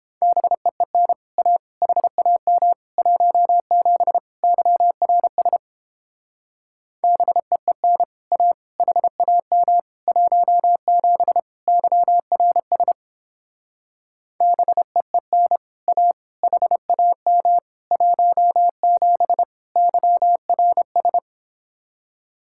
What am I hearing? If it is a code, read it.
Morse code, heard as a keyed tone, about 29 wpm, 700 Hz. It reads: BEEN A HAM 17 YRS BEEN A HAM 17 YRS BEEN A HAM 17 YRS